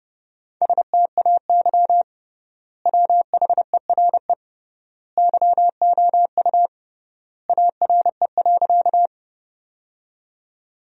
Morse code, heard as a keyed tone, about 30 wpm, 700 Hz. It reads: STAY WHERE YOU ARE.
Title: 3x Word-Spacing - Sentences from Top 500 Words - 30wpm